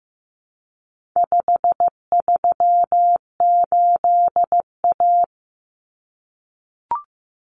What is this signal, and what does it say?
Morse code, heard as a keyed tone, about 15 wpm, 700 Hz. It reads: 538A
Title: Common Cut Numbers - 4-Digit 15wpm